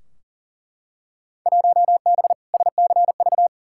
Morse code, heard as a keyed tone, about 40 wpm, 700 Hz. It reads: 1B SCV
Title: Field Day Exchanges - 40wpm